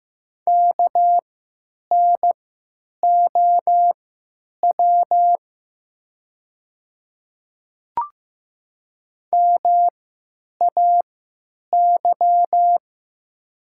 Morse code, heard as a keyed tone, about 15 wpm, spaced as for 8 wpm, 700 Hz. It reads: KNOW MAY
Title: 3x Character Spacing - Top 100 Words - 15wpm